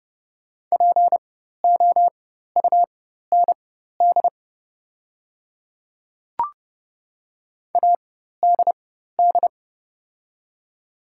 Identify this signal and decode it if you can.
Morse code, heard as a keyed tone, about 30 wpm, spaced as for 14 wpm, 690 Hz. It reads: POUND ADD
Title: 4x Character Spacing - Top 500 Words - 30wpm